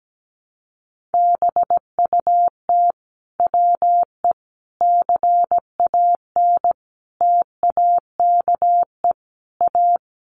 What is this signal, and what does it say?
Morse code, heard as a keyed tone, about 17 wpm, 700 Hz. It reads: BUT WE CAN TAKE A
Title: A Sound Of Thunder 17wpm